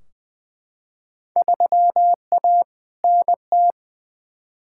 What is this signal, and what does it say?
Morse code, heard as a keyed tone, about 20 wpm, 700 Hz. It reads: 3A NT
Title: Field Day Exchanges Encore - 20wpm